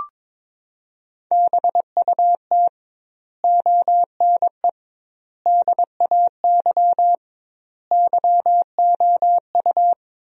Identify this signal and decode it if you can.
Morse code, heard as a keyed tone, about 22 wpm, 700 Hz. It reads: BUT ONE DAY YOU
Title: 2x Word-Spacing - Sentences from Top 400 Words - 22wpm